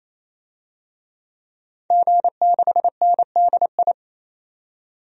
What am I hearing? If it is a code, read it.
Morse code, heard as a keyed tone, about 28 wpm, 700 Hz. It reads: G6NDI
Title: DXCC-100 Call Signs - 5 Characters - 28wpm